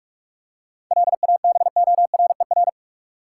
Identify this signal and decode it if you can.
Morse code, heard as a keyed tone, about 46 wpm, 700 Hz. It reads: LABORER
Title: K1USN SST QSO Practice 45wpm